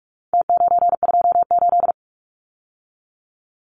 Morse code, heard as a keyed tone, about 44 wpm, 700 Hz. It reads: T928
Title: Numbers - Contest Style - 4-Digit 45wpm